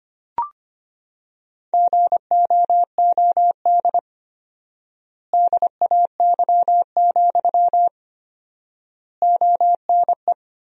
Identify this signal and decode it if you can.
Morse code, heard as a keyed tone, about 25 wpm, 700 Hz. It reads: GOOD DAY, ONE
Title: 4x Word-Spacing - Sentences from Top 200 Words - 25wpm